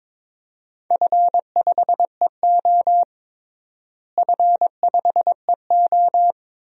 Morse code, heard as a keyed tone, about 22 wpm, 700 Hz. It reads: F5EO F5EO
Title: DXCC-100 Call Signs - 4 Characters - Repeated 3x - 22wpm